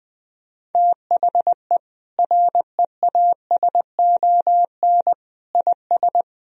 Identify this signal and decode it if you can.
Morse code, heard as a keyed tone, about 21 wpm, 700 Hz. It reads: THE REASON IS
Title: Sentences from Top 1000 Words - Review Part 1 - 20wpm